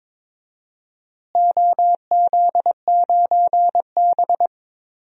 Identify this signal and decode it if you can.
Morse code, heard as a keyed tone, about 22 wpm, 700 Hz. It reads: OZ9B
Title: DXCC-100 Call Signs - 4 Characters - Repeated 2x - 22wpm